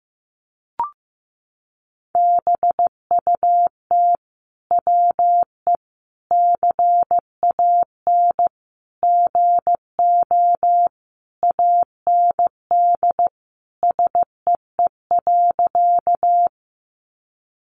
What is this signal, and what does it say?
Morse code, heard as a keyed tone, about 15 wpm, 700 Hz. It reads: BUT WE CAN GO AND SEE.
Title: Sentences from Top 100 Words 15wpm